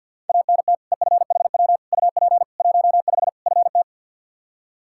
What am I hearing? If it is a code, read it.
Morse code, heard as a keyed tone, about 52 wpm, 700 Hz. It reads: ANT EFHW UP 15 FT
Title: Practice QSO - Exchange 2 - Info - 50wpm